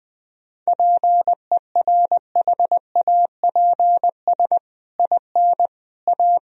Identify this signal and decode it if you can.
Morse code, heard as a keyed tone, about 21 wpm, 700 Hz. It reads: PERHAPS IN A
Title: Sentences from Top 1000 Words - Review Part 3 - 20wpm